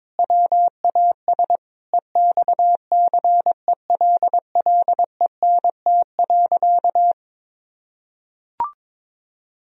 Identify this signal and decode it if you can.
Morse code, heard as a keyed tone, about 22 wpm, 700 Hz. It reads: WAS EXCELLENT.
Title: Sentences from Top 1000 Words - Review Part 4 - 22wpm